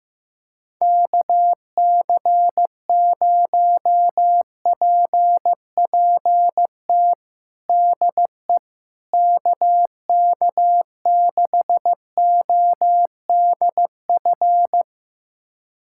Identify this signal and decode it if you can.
Morse code, heard as a keyed tone, about 15 wpm, 700 Hz. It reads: KC0PPT DE KK6ODF
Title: USA Callsign DE Callsign 15wpm